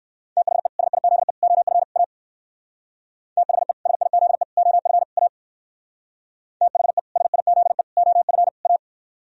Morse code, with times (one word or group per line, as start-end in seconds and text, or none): 0.4-0.7 THE
0.8-1.3 SIZE
1.4-1.8 OF
2.0-2.0 A
3.4-3.7 THE
3.9-4.4 SIZE
4.6-5.0 OF
5.2-5.3 A
6.6-7.0 THE
7.2-7.8 SIZE
8.0-8.5 OF
8.7-8.8 A